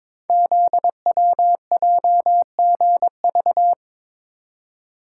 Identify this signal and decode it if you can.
Morse code, heard as a keyed tone, about 22 wpm, 700 Hz. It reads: ZWJGV
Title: Code Groups - 5 Letters - 22wpm